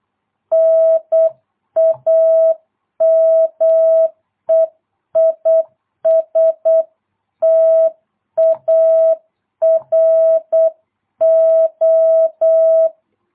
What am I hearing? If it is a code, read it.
Morse code, heard as a keyed tone, about 8 wpm, 650 Hz. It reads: NAME IS TARO